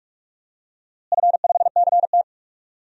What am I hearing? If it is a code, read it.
Morse code, heard as a keyed tone, about 47 wpm, 700 Hz. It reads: F5CT
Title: DXCC-100 Call Signs - 4 Characters - Repeated 2x - 45wpm